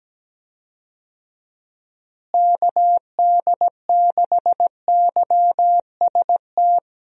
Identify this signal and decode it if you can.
Morse code, heard as a keyed tone, about 17 wpm, 700 Hz. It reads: KD6YST
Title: USA Call Signs - 6 Characters - Repeated 3x - 17wpm